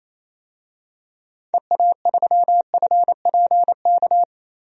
Morse code, heard as a keyed tone, about 28 wpm, 700 Hz. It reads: EA3FPK
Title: DXCC-100 Call Signs - 6 Characters - Repeated 2x - 28wpm